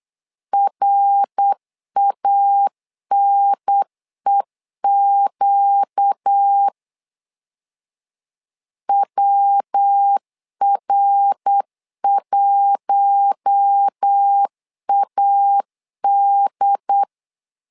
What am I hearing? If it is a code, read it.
Morse code, heard as a keyed tone, about 8 wpm, 800 Hz. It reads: RANEQ WR1AD
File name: LEZIONI / 4 Q1 8wpm 3035